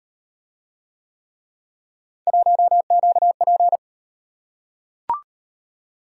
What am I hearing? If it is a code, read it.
Morse code, heard as a keyed tone, about 38 wpm, 700 Hz. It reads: 1QP